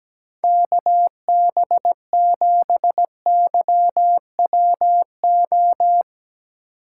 Morse code, heard as a keyed tone, about 17 wpm, 700 Hz. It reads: KB7YWO
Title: USA Call Signs - 6 Characters - 3x Character Spacing - 17wpm